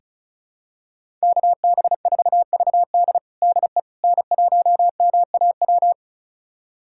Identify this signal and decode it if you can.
Morse code, heard as a keyed tone, about 36 wpm, 700 Hz. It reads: KB4VD DE N1MAW
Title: USA Callsign DE Callsign - 5 Characters - 35wpm